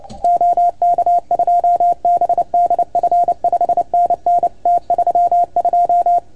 Morse code, heard as a keyed tone, about 28 wpm, 690 Hz. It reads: OK2BDF 5NN T32
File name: Archiv / CUC310 / OK1FGD CUC 310 pileup